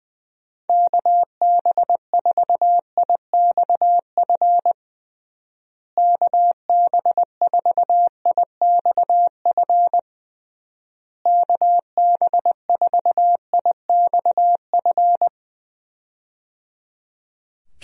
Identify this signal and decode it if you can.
Morse code, heard as a keyed tone, about 20 wpm, 700 Hz. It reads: KB4IXF KB4IXF KB4IXF
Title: USA Call Signs - 6 Characters - Repeated 3x - 20wpm